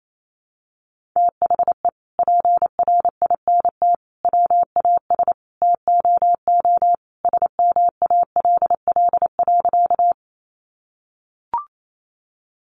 Morse code, heard as a keyed tone, about 28 wpm, 700 Hz. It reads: THE PRINT WAS TOO SMALL.
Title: Sentences from Top 1400 Words 28wpm